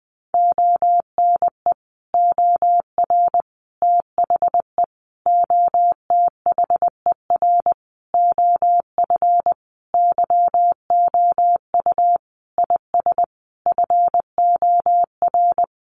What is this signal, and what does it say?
Morse code, heard as a keyed tone, about 20 wpm, 700 Hz. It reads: ONE OR THE OTHER OF YOU IS FOR
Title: Sentences from Top 400 Words 20wpm